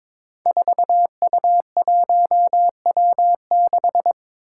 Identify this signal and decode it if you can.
Morse code, heard as a keyed tone, about 22 wpm, 700 Hz. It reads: 4U1W6
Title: Five Challenging Letter-Number Combinations 22wpm